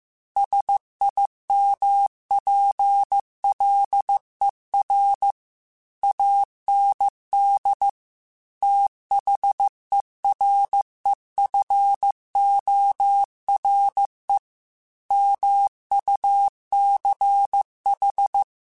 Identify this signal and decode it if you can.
Morse code, heard as a keyed tone, about 15 wpm, 800 Hz. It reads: SIMPLER AND THEREFORE MUCH
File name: Technical Subjects / Technical Subjects-mp3s / AND Gate @13